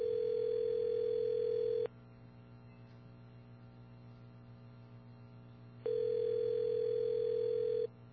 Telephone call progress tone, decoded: ringback tone